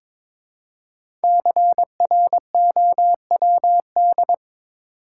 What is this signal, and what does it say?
Morse code, heard as a keyed tone, about 22 wpm, 700 Hz. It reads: CROWD
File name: 5 Letter Words - First 3 Letters Given - 22wpm